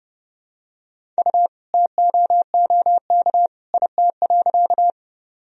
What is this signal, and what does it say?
Morse code, heard as a keyed tone, about 32 wpm, 700 Hz. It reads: U TOOK IT.
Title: Sentences from Top 300 Words and Abbreviations 30wpm